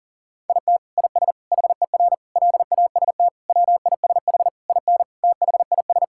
Morse code, heard as a keyed tone, about 40 wpm, 700 Hz. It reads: IT IS HER LAST WISH IN THIS
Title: Sentences from Top 1600 Words 40wpm